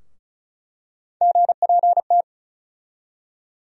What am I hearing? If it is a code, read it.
Morse code, heard as a keyed tone, about 36 wpm, 700 Hz. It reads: GPT
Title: USA Call Sign Postfixes - 3 Letters 35wpm